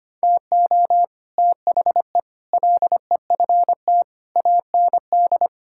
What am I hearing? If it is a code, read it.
Morse code, heard as a keyed tone, about 25 wpm, 700 Hz. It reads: TO THE LEFT AND
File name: practice / 25wpm / Sentences with 9-Word Phrases - 25wpm